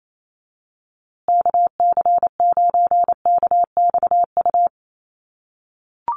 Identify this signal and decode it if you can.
Morse code, heard as a keyed tone, about 28 wpm, 700 Hz. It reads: KC9KXU
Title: USA Call Signs Repeated 3x 28wpm